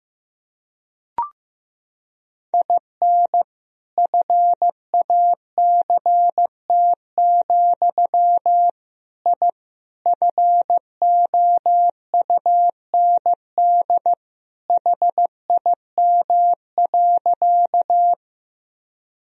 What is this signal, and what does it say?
Morse code, heard as a keyed tone, about 15 wpm, 700 Hz. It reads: IN FACT, I FOUND HIM.
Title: Sentences from Top 1000 Words - Review Part 3 - 15wpm